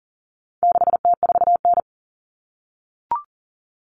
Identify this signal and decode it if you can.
Morse code, heard as a keyed tone, about 40 wpm, 700 Hz. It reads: 6T4N
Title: Common Cut Numbers - 4-Digit 40wpm